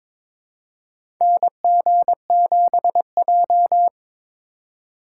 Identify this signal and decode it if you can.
Morse code, heard as a keyed tone, about 22 wpm, 700 Hz. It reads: NG7J
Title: USA Call Signs - 2x Character Spacing - 22wpm